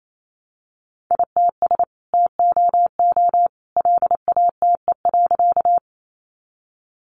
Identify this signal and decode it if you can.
Morse code, heard as a keyed tone, about 28 wpm, 700 Hz. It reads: ITS TOO LATE.
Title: Sentences from Top 600 Words 28wpm